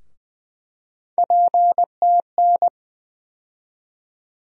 Morse code, heard as a keyed tone, about 20 wpm, 700 Hz. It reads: PTN
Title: USA Call Sign Postfixes - 3 Letters 20wpm